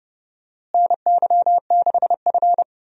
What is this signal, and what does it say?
Morse code, heard as a keyed tone, about 29 wpm, 700 Hz. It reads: NY6F
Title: USA Call Signs - 4 Characters - Repeated 3x - 30wpm